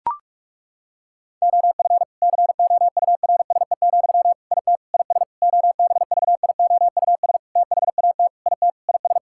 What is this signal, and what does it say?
Morse code, heard as a keyed tone, about 41 wpm, 700 Hz. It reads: OF COURSE, IT IS OBVIOUS THAT IT IS
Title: Sentences from Top 1600 Words 45wpm